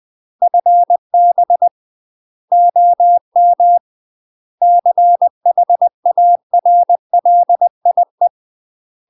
Morse code, heard as a keyed tone, about 20 wpm, 700 Hz. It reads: FB OM CHARLIE
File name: UK MASTER QSOs TRAINING / 20WPM / UK MASTER QSOs 0055